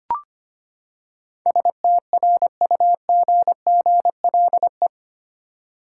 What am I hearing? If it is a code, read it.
Morse code, heard as a keyed tone, about 25 wpm, 700 Hz. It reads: STRUGGLE
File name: Eight Letter Words - 25wpm